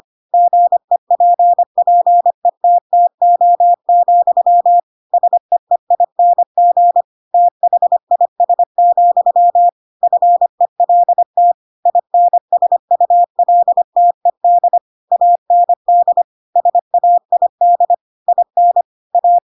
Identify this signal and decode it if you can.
Morse code, heard as a keyed tone, about 25 wpm, 700 Hz. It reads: GEPPETTO, SEEING THIS, FELT INSULTED AND SAID IN A
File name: Pinocchio / Pinocchio ch3 25 25 ik1hge